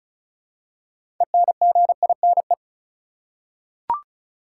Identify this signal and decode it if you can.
Morse code, heard as a keyed tone, about 34 wpm, 700 Hz. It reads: ENGINE
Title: Six Letter Words - 35wpm